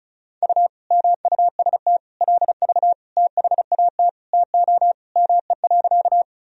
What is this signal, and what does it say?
Morse code, heard as a keyed tone, about 34 wpm, 700 Hz. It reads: U MUST LV THAT TO ME.
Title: Sentences from Top 400 Words and Abbreviations 35wpm